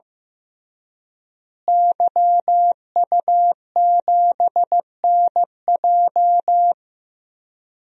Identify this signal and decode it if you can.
Morse code, heard as a keyed tone, about 15 wpm, 700 Hz. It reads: YU7NJ
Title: DXCC-100 Call Signs - 5 Characters - Repeated 2x - 15wpm